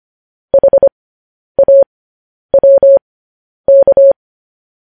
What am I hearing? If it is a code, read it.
Morse code, heard as a keyed tone, about 25 wpm, spaced as for 10 wpm, 550 Hz. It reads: HAWK
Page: news headlines in morse code at 10 wpm